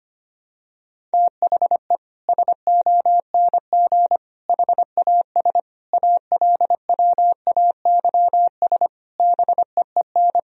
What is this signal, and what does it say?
Morse code, heard as a keyed tone, about 25 wpm, 700 Hz. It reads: THE SONG HAS ALWAYS BEEN
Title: Sentences from Top 1000 Words - Review Part 3 - 25wpm